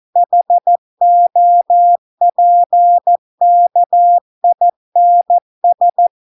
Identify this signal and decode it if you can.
Morse code, heard as a keyed tone, about 14 wpm, 700 Hz. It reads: HOPKINS